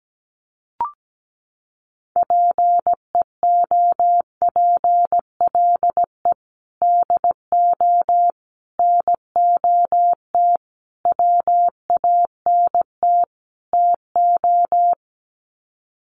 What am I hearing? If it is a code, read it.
Morse code, heard as a keyed tone, about 17 wpm, 700 Hz. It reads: PEOPLE DO NOT WANT TO
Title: Sets of 5 Words Encore2 17wpm